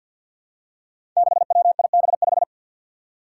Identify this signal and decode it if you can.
Morse code, heard as a keyed tone, about 46 wpm, 700 Hz. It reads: 6WIB5
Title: Five Challenging Letter-Number Combinations 50wpm